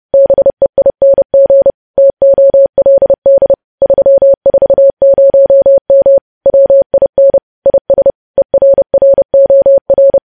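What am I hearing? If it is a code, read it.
Morse code, heard as a keyed tone, about 30 wpm, 550 Hz. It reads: BEING TOLD 340M WIN IS ERROR
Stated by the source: News headlines sent in Morse at 30.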